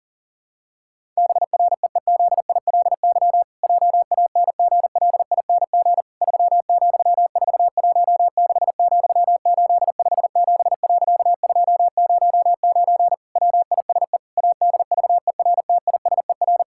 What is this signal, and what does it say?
Morse code, heard as a keyed tone, about 40 wpm, 700 Hz. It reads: BREEZILY JANGLING 3,416,857.209 WISE ADVERTISER
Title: Pangram Challenge v2 - 40wpm